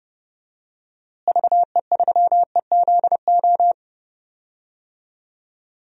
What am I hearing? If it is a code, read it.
Morse code, heard as a keyed tone, about 30 wpm, 700 Hz. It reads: VE3EZO
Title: DXCC-100 Call Signs - 6 Characters - Repeated 3x - 30wpm